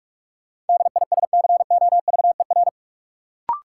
Morse code, heard as a keyed tone, about 41 wpm, 700 Hz. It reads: DISCOVER
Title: Sentences from Top 1100 Words - New Word Drill - 45wpm